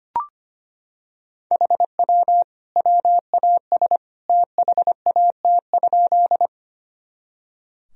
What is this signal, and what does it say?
Morse code, heard as a keyed tone, about 25 wpm, 700 Hz. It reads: HW WAS THAT?
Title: Sentences from Top 100 Words and Abbreviations 25wpm